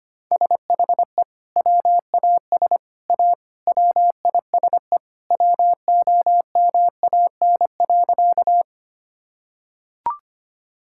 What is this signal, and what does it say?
Morse code, heard as a keyed tone, about 25 wpm, 700 Hz. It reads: SHE WAS A WISE WOMAN.
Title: Sentences from Top 1600 Words 25wpm